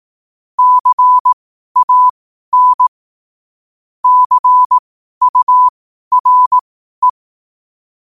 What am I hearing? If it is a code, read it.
Morse code, heard as a keyed tone, about 18 wpm, spaced as for 12 wpm, 1000 Hz. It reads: CAN CURE